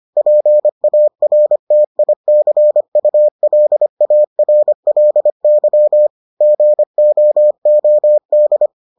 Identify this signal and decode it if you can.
Morse code, heard as a keyed tone, about 25 wpm, 600 Hz. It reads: PARTICULARLY GOOD